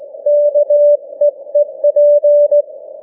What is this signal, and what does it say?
Morse code, heard as a keyed tone, about 17 wpm, 590 Hz. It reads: KEEP
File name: cw / ssn 40M